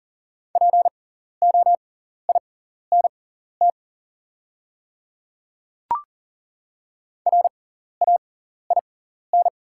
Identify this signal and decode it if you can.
Morse code, heard as a keyed tone, about 40 wpm, spaced as for 14 wpm, 700 Hz. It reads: POINT RAIN
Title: 6x Character Spacing - Top 300 Words - 40wpm